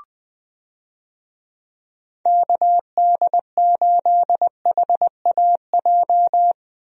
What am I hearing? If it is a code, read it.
Morse code, heard as a keyed tone, about 21 wpm, 700 Hz. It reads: KD8HAJ
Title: USA Call Signs - 6 Characters - Repeated 3x - 20wpm